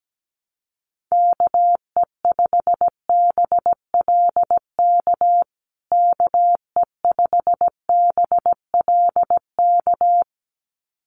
Morse code, heard as a keyed tone, about 17 wpm, 700 Hz. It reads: KE5BLK KE5BLK
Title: USA Call Signs Repeated 2x 17wpm